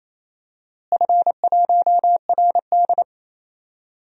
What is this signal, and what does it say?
Morse code, heard as a keyed tone, about 28 wpm, 700 Hz. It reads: F1RD
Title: DXCC-100 Call Signs - 4 Characters - Repeated 3x - 28wpm